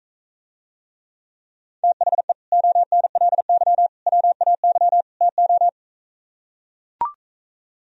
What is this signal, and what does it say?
Morse code, heard as a keyed tone, about 43 wpm, 690 Hz. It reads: THE ONLY WAY TO